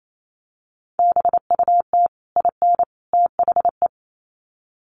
Morse code, heard as a keyed tone, about 28 wpm, 700 Hz. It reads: BUT IN THE
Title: Sets of 3 Words 28wpm